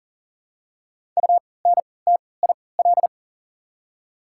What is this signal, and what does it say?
Morse code, heard as a keyed tone, about 40 wpm, spaced as for 23 wpm, 700 Hz. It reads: UNTIL
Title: 3x Character Spacing - Top 300 Words - 40wpm